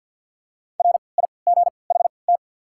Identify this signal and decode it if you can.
Morse code, heard as a keyed tone, about 47 wpm, 700 Hz. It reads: R I G H T
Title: 3x Character Spacing - Top 200 Words - 50wpm